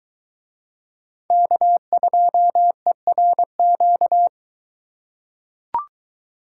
Morse code, heard as a keyed tone, about 22 wpm, 700 Hz. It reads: K2ERQ